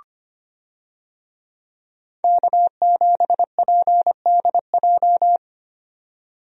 Morse code, heard as a keyed tone, about 25 wpm, 700 Hz. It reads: K7PDJ
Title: USA Call Signs - 5 Characters - 25wpm